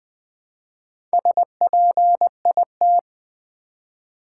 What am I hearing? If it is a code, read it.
Morse code, heard as a keyed tone, about 20 wpm, 700 Hz. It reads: SPIT
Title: Build A Word - 1 Letter At A Time - 5-Letter Words - 20wpm